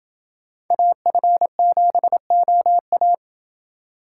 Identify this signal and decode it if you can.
Morse code, heard as a keyed tone, about 28 wpm, 700 Hz. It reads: AF7OA